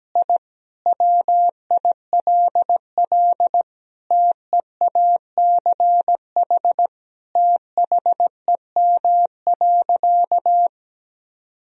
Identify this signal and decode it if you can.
Morse code, heard as a keyed tone, about 17 wpm, 700 Hz. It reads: I WILL TEACH THEM.
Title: Sentences from Top 1000 Words - Review Part 2 - 17wpm